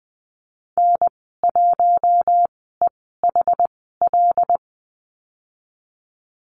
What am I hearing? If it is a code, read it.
Morse code, heard as a keyed tone, about 20 wpm, spaced as for 15 wpm, 700 Hz. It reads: N1EHL
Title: 088 Lesson 15g - Number 5 - Call Signs 20wpm